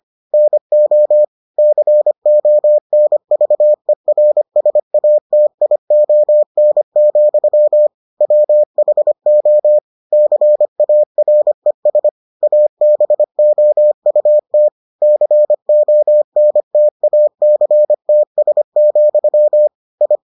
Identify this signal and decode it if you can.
Morse code, heard as a keyed tone, about 25 wpm, 600 Hz. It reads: NO CONVERSATION, WHO CARES ABOUT CONTACTS, I